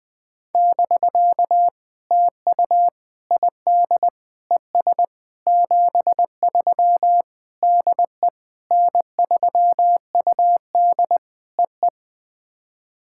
Morse code, heard as a keyed tone, about 18 wpm, 700 Hz. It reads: <BK> TU ID ES 73 DE N3UD EE
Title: POTA QSO Practice - 20wpm